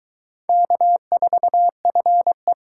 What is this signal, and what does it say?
Morse code, heard as a keyed tone, about 22 wpm, 700 Hz. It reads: K4FE